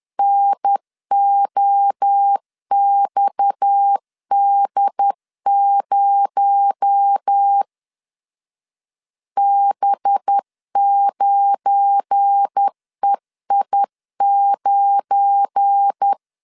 Morse code, heard as a keyed tone, about 10 wpm, 800 Hz. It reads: NOXD0 B9EI9
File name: LEZIONI / 7 GF9Z / 7 GF9 10wpm 2747